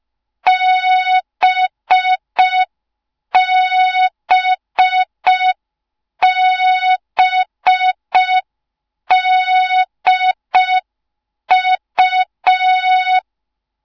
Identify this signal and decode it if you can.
Morse code, heard as a keyed tone, about 5 wpm, 3700 Hz. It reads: BBBDU
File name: Morze-HA2MN / hang / 06BDU HA2MN